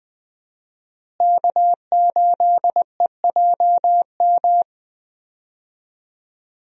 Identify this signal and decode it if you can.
Morse code, heard as a keyed tone, about 20 wpm, 700 Hz. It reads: K8EJM
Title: USA Call Signs - 5 Characters - Repeated 2x - 20wpm